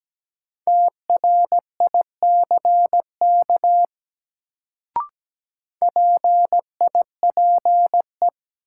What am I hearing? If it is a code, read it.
Morse code, heard as a keyed tone, about 17 wpm, 700 Hz. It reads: TRICK PIPE
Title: Sentences from Top 1500 Words - New Word Drill - 17wpm